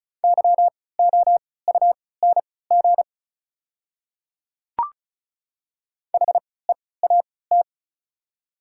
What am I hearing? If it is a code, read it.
Morse code, heard as a keyed tone, about 36 wpm, spaced as for 20 wpm, 690 Hz. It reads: YOUNG HEAT
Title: 3x Character Spacing - Top 500 Words - 35wpm